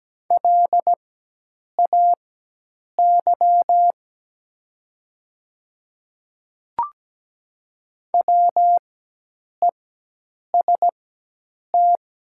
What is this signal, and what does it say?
Morse code, heard as a keyed tone, about 17 wpm, spaced as for 8 wpm, 690 Hz. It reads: LAY WEST
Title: 4x Character Spacing - Top 500 Words - 17wpm